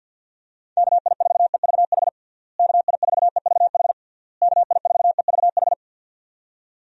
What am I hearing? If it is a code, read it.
Morse code, heard as a keyed tone, about 51 wpm, 700 Hz. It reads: KI4EVH KI4EVH KI4EVH
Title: USA Call Signs - 6 Characters - Repeated 3x - 50wpm